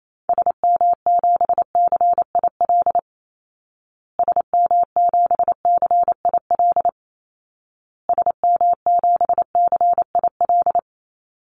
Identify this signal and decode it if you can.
Morse code, heard as a keyed tone, about 28 wpm, 700 Hz. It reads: SM7CIL SM7CIL SM7CIL
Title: Call Signs Repeated 2x - 28wpm